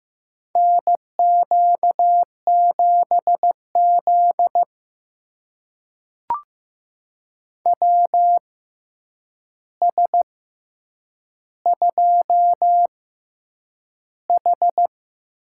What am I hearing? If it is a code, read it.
Morse code, heard as a keyed tone, about 15 wpm, spaced as for 8 wpm, 700 Hz. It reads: NQ7Z WS2H
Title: USA Call Signs - 6x Character Spacing - 15wpm